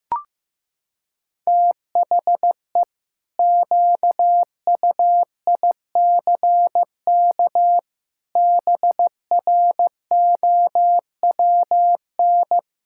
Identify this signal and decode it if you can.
Morse code, heard as a keyed tone, about 15 wpm, 700 Hz. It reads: THE QUICK BROWN
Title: Pangram Challenge v2 - 15wpm